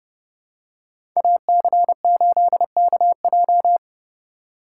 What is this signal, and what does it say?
Morse code, heard as a keyed tone, about 30 wpm, 700 Hz. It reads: AC8KJ
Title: USA Call Signs - 5 Characters - 3x Character Spacing - 30wpm